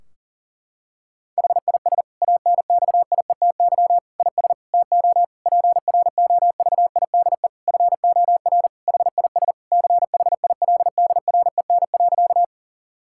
Morse code, heard as a keyed tone, about 40 wpm, 700 Hz. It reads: HIS ANXIETY IS TO PROVIDE FOR HIS CHILDREN.
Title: Sentences from Top 1600 Words 40wpm